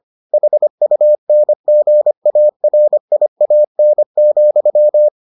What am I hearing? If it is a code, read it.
Morse code, heard as a keyed tone, about 25 wpm, 600 Hz. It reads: HUNGARIAN,